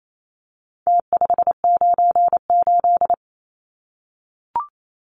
Morse code, heard as a keyed tone, about 28 wpm, 700 Hz. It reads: T598
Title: Numbers - Contest Style - 4-Digit 28wpm